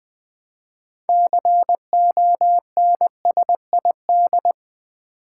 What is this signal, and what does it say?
Morse code, heard as a keyed tone, about 20 wpm, 700 Hz. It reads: CONSID
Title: Build A Word - 2 Letters At A Time - 8-Letter Words - 20wpm